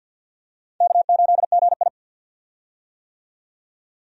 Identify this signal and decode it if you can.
Morse code, heard as a keyed tone, about 47 wpm, 700 Hz. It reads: K8GI
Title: USA Call Signs - 4 Characters - 50wpm